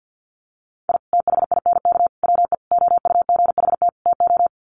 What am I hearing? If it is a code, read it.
Morse code, heard as a keyed tone, about 49 wpm, 700 Hz. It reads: I THINK WE OUGHT TO
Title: Sets of 5 Words Encore 50wpm